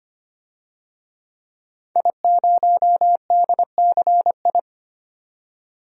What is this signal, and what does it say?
Morse code, heard as a keyed tone, about 25 wpm, 700 Hz. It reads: I0DCI
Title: DXCC-100 Call Signs - 5 Characters - 25wpm